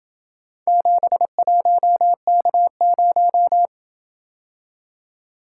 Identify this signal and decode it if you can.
Morse code, heard as a keyed tone, about 27 wpm, 700 Hz. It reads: 71K0